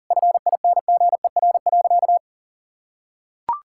Morse code, heard as a keyed tone, about 40 wpm, 700 Hz. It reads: FINGER.
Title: Sentences from Top 1600 Words 40wpm